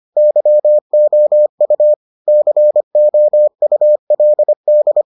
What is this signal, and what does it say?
Morse code, heard as a keyed tone, about 25 wpm, 600 Hz. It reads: YOU COULD